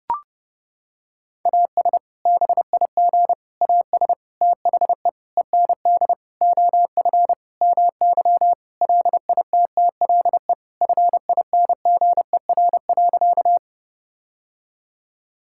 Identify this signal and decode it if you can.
Morse code, heard as a keyed tone, about 30 wpm, 700 Hz. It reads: AS BIG AS THE END OF MY LITTLE FINGER.
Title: Sentences from Top 1600 Words 30wpm